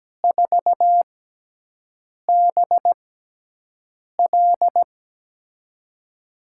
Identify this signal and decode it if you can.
Morse code, heard as a keyed tone, about 17 wpm, spaced as for 6 wpm, 700 Hz. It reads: 4BL
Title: USA Call Signs - 5 Characters - 6x Character Spacing - 17wpm